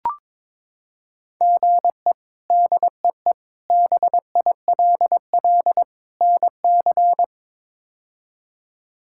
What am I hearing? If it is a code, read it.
Morse code, heard as a keyed tone, about 21 wpm, 700 Hz. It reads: GE DEE BILL NC
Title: K1USN SST QSO Practice - 22wpm